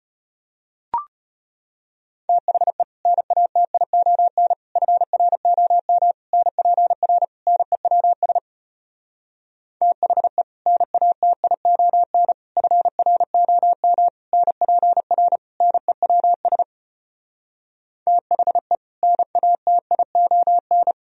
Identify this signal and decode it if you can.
Morse code, heard as a keyed tone, about 33 wpm, 700 Hz. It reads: THE NATION FROM NPR NEWS THE NATION FROM NPR NEWS THE NATION